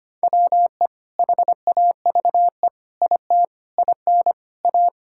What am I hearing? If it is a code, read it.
Morse code, heard as a keyed tone, about 25 wpm, 700 Hz. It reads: WE HAVE IT IN A
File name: practice / 25wpm / Sentences with 4-Word Phrases - 25wpm